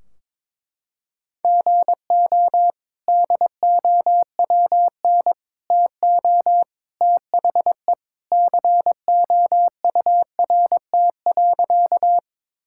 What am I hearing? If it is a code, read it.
Morse code, heard as a keyed tone, about 22 wpm, 700 Hz. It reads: GO DOWN TO THE COURT.
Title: Sentences from Top 1600 Words 22wpm